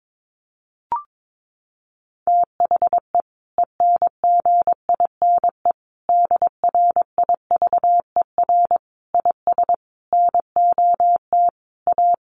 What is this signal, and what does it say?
Morse code, heard as a keyed tone, about 22 wpm, 700 Hz. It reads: THE ENGINE DRIVER IS NOT A
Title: Sentences from Top 1500 Words 22wpm